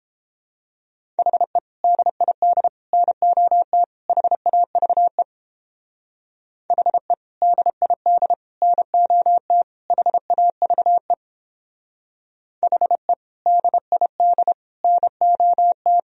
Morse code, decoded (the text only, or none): HE DID NOT HAVE HE DID NOT HAVE HE DID NOT